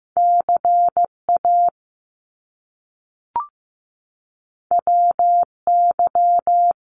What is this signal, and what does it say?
Morse code, heard as a keyed tone, about 15 wpm, 700 Hz. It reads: CA WY
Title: US States 15wpm